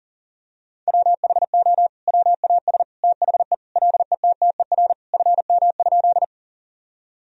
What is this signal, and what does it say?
Morse code, heard as a keyed tone, about 37 wpm, 700 Hz. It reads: WHO WAS THE LETTER FM?
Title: Sentences from Top 300 Words and Abbreviations 40wpm